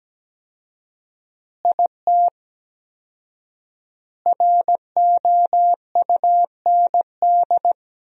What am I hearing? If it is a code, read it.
Morse code, heard as a keyed tone, about 17 wpm, 700 Hz. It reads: IT ROUND
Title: 4x Word-Spacing - Sentences from Top 500 Words - 17wpm